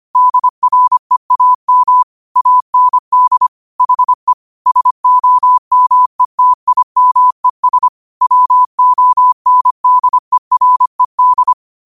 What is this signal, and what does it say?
Morse code, heard as a keyed tone, about 25 wpm, 1000 Hz. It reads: DREAM AND HE SOMETIMES WONDERED
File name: 25wpm